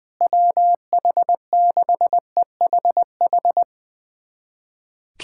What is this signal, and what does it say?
Morse code, heard as a keyed tone, about 20 wpm, 700 Hz. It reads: WH6EHH
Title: USA Callsign DE Callsign - 6 Characters - 20wpm